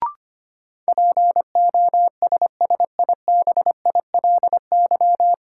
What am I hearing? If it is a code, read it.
Morse code, heard as a keyed tone, about 25 wpm, 700 Hz. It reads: POSSIBILY
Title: Sentences from Top 1300 Words - New Word Drill - 25wpm